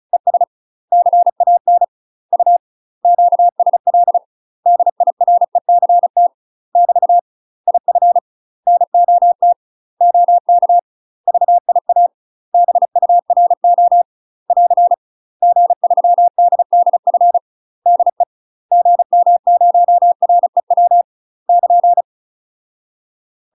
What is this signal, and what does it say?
Morse code, heard as a keyed tone, about 34 wpm, 700 Hz. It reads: ES CAN U QSL DIRECT <BT> IF NOT OK VIA BURO <AR> G3DDF DE GM0REW <KN>
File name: UK MASTER QSOs TRAINING / 35WPM / UK MASTER QSOs 0030